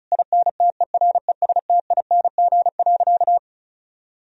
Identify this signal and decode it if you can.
Morse code, heard as a keyed tone, about 34 wpm, 700 Hz. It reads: INTERESTING.
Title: Sentences from Top 1000 Words - Review Part 4 - 35wpm